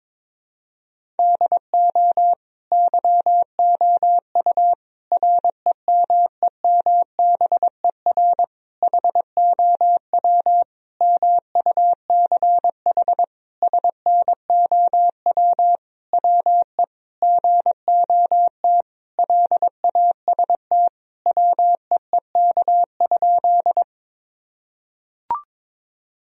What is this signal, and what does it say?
Morse code, heard as a keyed tone, about 22 wpm, 700 Hz. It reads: DO YOU REMEMBER HOW MUCH SNOW WE GOT LAST WEEK?
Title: Sentences from Top 1000 Words - Review Part 3 - 22wpm